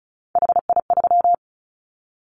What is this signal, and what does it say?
Morse code, heard as a keyed tone, about 34 wpm, 700 Hz. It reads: HI3
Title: Callsign Prefixes for Top 100 DXCC - 35wpm